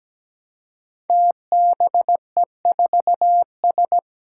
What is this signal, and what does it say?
Morse code, heard as a keyed tone, about 17 wpm, 700 Hz. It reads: TBE4S
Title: Five Challenging Letter-Number Combinations 17wpm